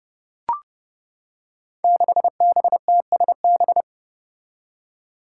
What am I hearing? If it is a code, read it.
Morse code, heard as a keyed tone, about 30 wpm, 700 Hz. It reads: 6BTSB
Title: Five Challenging Letter-Number Combinations 30wpm